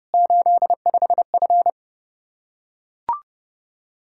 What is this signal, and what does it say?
Morse code, heard as a keyed tone, about 32 wpm, 700 Hz. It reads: 85F
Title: Practice QSO - Exchange 2 - Info - 30wpm